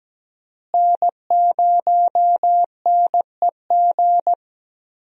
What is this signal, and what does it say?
Morse code, heard as a keyed tone, about 17 wpm, 700 Hz. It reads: N0NEG
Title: USA Call Signs - 5 Characters - Repeated 3x - 17wpm